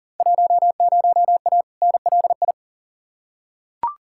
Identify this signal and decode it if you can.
Morse code, heard as a keyed tone, about 40 wpm, 700 Hz. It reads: 10A NLI
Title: Field Day Exchanges Encore - 40wpm